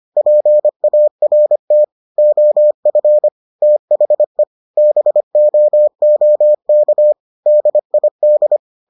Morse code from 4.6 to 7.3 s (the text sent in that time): BOOK